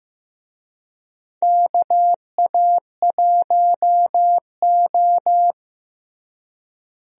Morse code, heard as a keyed tone, about 15 wpm, 700 Hz. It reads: KA1O
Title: USA Call Signs - 4 Characters - Repeated 3x - 15wpm